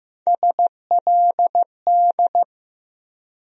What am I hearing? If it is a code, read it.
Morse code, heard as a keyed tone, about 15 wpm, 700 Hz. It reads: SLD
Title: USA Call Sign Postfixes - 3 Letters 15wpm